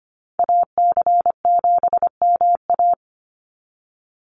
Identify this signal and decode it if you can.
Morse code, heard as a keyed tone, about 25 wpm, 700 Hz. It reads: AC7MA
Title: NA SOTA Activator Callsigns 25wpm